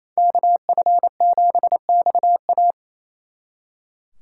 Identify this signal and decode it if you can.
Morse code, heard as a keyed tone, about 37 wpm, 700 Hz. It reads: KF7XA